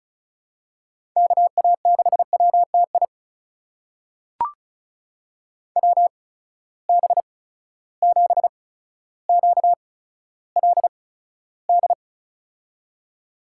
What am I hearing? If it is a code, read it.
Morse code, heard as a keyed tone, about 34 wpm, spaced as for 15 wpm, 700 Hz. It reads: KA6WTI WB7QLD
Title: USA Call Signs - 6 Characters - 8x Character Spacing - 35wpm